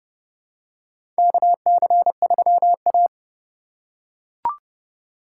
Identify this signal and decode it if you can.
Morse code, heard as a keyed tone, about 30 wpm, 700 Hz. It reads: KC3A
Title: USA Call Signs - 4 Characters - 8x Character Spacing - 30wpm